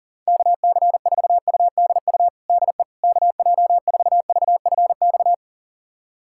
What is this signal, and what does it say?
Morse code, heard as a keyed tone, about 40 wpm, 700 Hz. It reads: KC4UDU DE KJ4VFX
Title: USA Callsign DE Callsign - 6 Characters - 40wpm